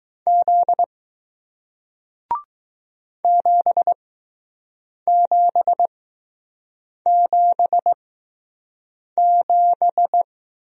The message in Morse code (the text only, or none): Z 7777